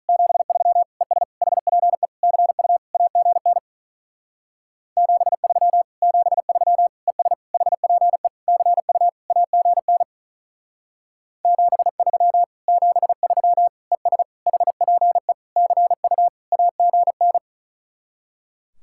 Morse code, heard as a keyed tone, about 45 wpm, 700 Hz. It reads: 73 ES HPE CU AGN 73 73 ES HPE CU AGN 73 73 ES HPE CU AGN